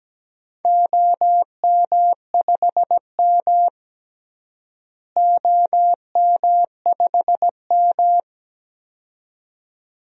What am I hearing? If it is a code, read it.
Morse code, heard as a keyed tone, about 17 wpm, 700 Hz. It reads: OM5M OM5M
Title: DXCC-100 Call Signs - 4 Characters - Repeated 2x - 17wpm